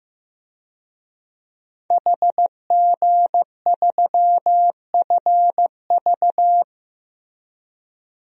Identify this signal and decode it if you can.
Morse code, heard as a keyed tone, about 15 wpm, 700 Hz. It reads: HG3FV
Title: DXCC-100 Call Signs - 5 Characters - Repeated 3x - 15wpm